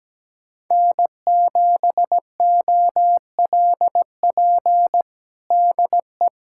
Morse code, heard as a keyed tone, about 17 wpm, 700 Hz. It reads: N7OLP DE
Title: USA Callsign DE Callsign - 5 Characters - 17wpm